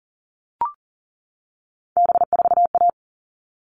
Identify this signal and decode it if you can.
Morse code, heard as a keyed tone, about 40 wpm, 700 Hz. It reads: B4A
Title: All Cut Numbers - 3-Digit 40wpm